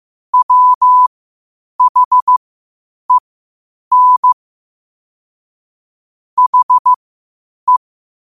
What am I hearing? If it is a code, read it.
Morse code, heard as a keyed tone, about 15 wpm, spaced as for 8 wpm, 1000 Hz. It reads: WHEN HE